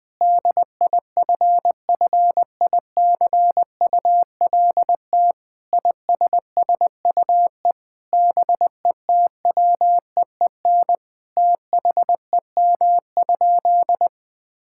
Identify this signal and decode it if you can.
Morse code, heard as a keyed tone, about 20 wpm, 700 Hz. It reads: DIFFICULT ISSUE BETWEEN THEM?
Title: Sentences from Top 1000 Words - Review Part 6 - 20wpm